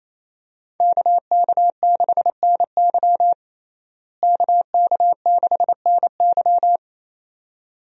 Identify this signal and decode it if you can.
Morse code, heard as a keyed tone, about 28 wpm, 700 Hz. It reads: KK6NY KK6NY
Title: USA Call Signs - 5 Characters - Repeated 2x - 28wpm